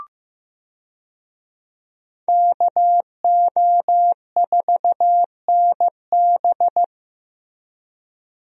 Morse code, heard as a keyed tone, about 15 wpm, 700 Hz. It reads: KO4NB
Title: USA Call Signs - 5 Characters - Repeated 3x - 15wpm